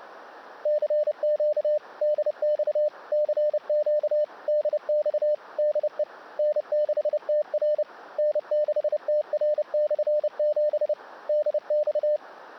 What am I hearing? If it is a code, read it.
Morse code, heard as a keyed tone, about 28 wpm, 600 Hz. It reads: CQ DX CQ DX DE N6TR N6TR/7 DX
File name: 35 30 WPM Remote